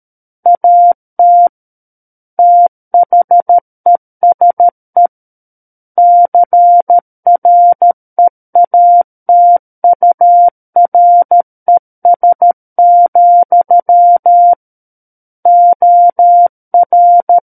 Morse code, heard as a keyed tone, about 13 wpm, 700 Hz. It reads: AT THESE CREATURES, OR
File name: Book One / 04-Chapter 4 - 13 WPM